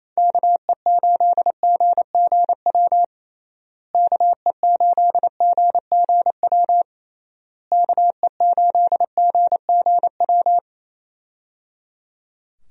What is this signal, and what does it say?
Morse code, heard as a keyed tone, about 28 wpm, 700 Hz. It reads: KE8GGW KE8GGW KE8GGW
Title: USA Call Signs - 6 Characters - Repeated 3x - 28wpm